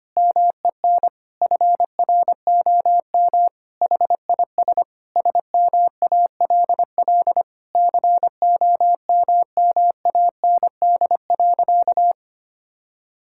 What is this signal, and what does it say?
Morse code, heard as a keyed tone, about 25 wpm, 700 Hz. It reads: MEN FROM HIS SMALL COMMAND.
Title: Sentences from Top 1600 Words 25wpm